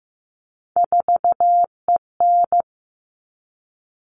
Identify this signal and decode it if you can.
Morse code, heard as a keyed tone, about 15 wpm, 700 Hz. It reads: 4EN
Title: All Cut Numbers - 3-Digit 15wpm